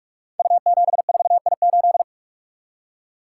Morse code, heard as a keyed tone, about 44 wpm, 700 Hz. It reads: U74I8
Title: Code Groups - 5 Letter:Numbers - 45wpm